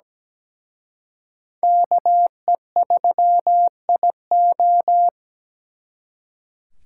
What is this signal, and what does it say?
Morse code, heard as a keyed tone, about 20 wpm, 700 Hz. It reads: KE3IO
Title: Speed Racing - USA Call Signs 17wpm